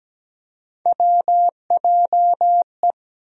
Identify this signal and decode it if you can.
Morse code, heard as a keyed tone, about 17 wpm, 700 Hz. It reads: WJE
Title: USA Call Sign Postfixes - 3 Letters 17wpm